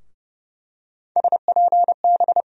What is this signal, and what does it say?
Morse code, heard as a keyed tone, about 32 wpm, 700 Hz. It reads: SPB
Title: USA Call Sign Postfixes - 3 Letters 30wpm